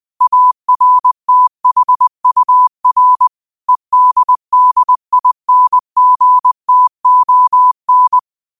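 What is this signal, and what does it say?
Morse code, heard as a keyed tone, about 20 wpm, 1000 Hz. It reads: ARTHUR EDDINGTON
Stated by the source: Morse code practice at 20 wpm: famous quotations sent in telegraphy, from URVAG.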